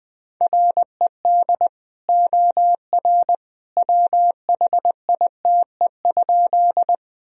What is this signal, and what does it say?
Morse code, heard as a keyed tone, about 20 wpm, 700 Hz. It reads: RED OR WHITE?
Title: Sentences from Top 1000 Words - Review Part 2 - 20wpm